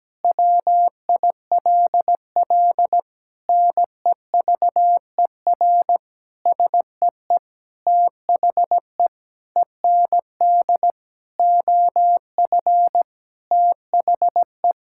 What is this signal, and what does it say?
Morse code, heard as a keyed tone, about 17 wpm, 700 Hz. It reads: WILL NEVER SEE THE END OF THE
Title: Sentences from Top 1600 Words 17wpm